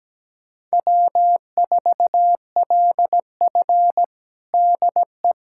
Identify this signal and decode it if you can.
Morse code, heard as a keyed tone, about 17 wpm, 700 Hz. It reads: W4LF DE
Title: USA Callsign DE Callsign - 4 Characters - 17wpm